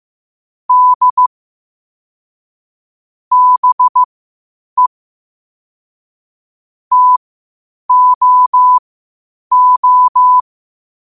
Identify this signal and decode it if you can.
Morse code, heard as a keyed tone, about 15 wpm, spaced as for 8 wpm, 1000 Hz. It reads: D BE TOO
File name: qotdfiles / 8wpm